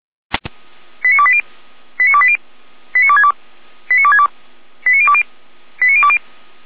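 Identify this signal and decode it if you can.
Five-tone selective call (ZVEI), heard as quick runs of five tones, tones of about 70 ms. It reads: 89280 89280 89262 89262 89020 89020
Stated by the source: ZVEI five-tone selective call received on SDR.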